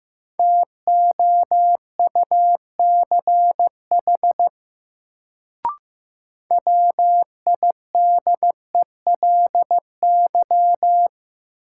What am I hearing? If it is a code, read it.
Morse code, heard as a keyed tone, about 15 wpm, 700 Hz. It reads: TOUCH WIDELY
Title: Sentences from Top 1200 Words - New Word Drill - 15wpm